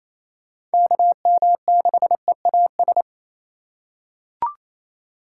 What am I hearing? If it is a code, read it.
Morse code, heard as a keyed tone, about 28 wpm, 700 Hz. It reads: KM6EAS
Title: USA Call Signs - 6 Characters - 2x Character Spacing - 28wpm